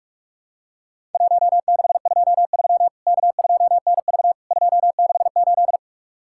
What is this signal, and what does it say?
Morse code, heard as a keyed tone, about 44 wpm, 700 Hz. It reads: 1623 K2NV 268
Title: Worked All Europe DX Contest - QTC Exchange - 45wpm